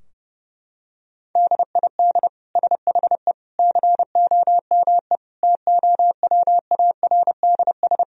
Morse code, heard as a keyed tone, about 30 wpm, 700 Hz. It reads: DID SHE COME TOWARDS
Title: Sentences from Top 1000 Words - Review Part 2 - 30wpm